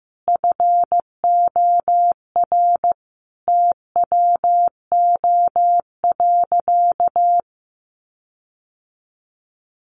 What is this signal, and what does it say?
Morse code, heard as a keyed tone, about 15 wpm, 700 Hz. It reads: FOR TWO.
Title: Sentences from Top 1400 Words 15wpm